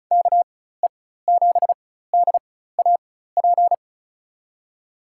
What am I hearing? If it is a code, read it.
Morse code, heard as a keyed tone, about 34 wpm, spaced as for 16 wpm, 700 Hz. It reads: KE7DAP
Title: USA Call Signs - 4x Character Spacing - 35wpm